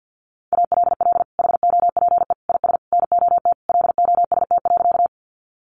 Morse code, heard as a keyed tone, about 50 wpm, 700 Hz. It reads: ALL HOPE IS NOT LOST.
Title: Sentences from Top 1400 Words 50wpm